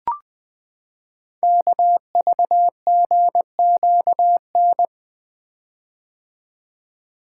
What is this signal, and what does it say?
Morse code, heard as a keyed tone, about 20 wpm, 700 Hz. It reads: KVGQN
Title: Code Groups - 5 Letters - 20wpm